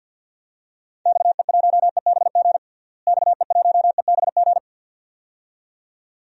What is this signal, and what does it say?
Morse code, heard as a keyed tone, about 53 wpm, 700 Hz. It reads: XE1EBG XE1EBG
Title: DXCC-100 Call Signs - 6 Characters - Repeated 2x - 50wpm